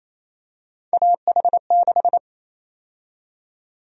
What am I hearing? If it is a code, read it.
Morse code, heard as a keyed tone, about 28 wpm, 700 Hz. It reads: AH6
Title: USA Call Sign Prefixes - Repeated 2x - 28wpm